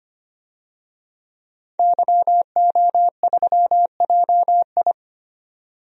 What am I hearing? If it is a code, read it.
Morse code, heard as a keyed tone, about 25 wpm, 700 Hz. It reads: YO3JI
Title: DXCC-100 Call Signs - 5 Characters - 25wpm